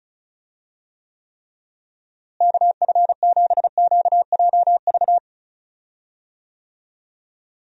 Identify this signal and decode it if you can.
Morse code, heard as a keyed tone, about 36 wpm, 700 Hz. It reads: KF7QJV
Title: USA Call Signs - 6 Characters - Repeated 3x - 35wpm